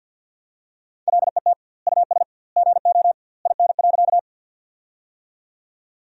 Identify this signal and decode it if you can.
Morse code, heard as a keyed tone, about 47 wpm, 700 Hz. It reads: LET US GO IN.
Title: 2x Word-Spacing - Sentences from Top 300 Words - 50wpm